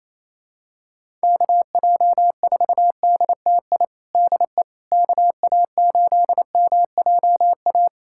Morse code, heard as a keyed tone, about 28 wpm, 700 Hz. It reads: KJ4DTI DE KA8MJA
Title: USA Callsign DE Callsign 28wpm